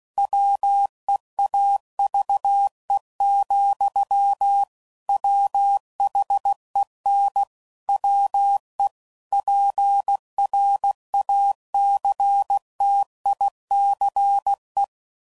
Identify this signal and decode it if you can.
Morse code, heard as a keyed tone, about 16 wpm, 800 Hz. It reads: WEAVE, WHEN WE PRACTICE
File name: Sayings / Quotes / Deceive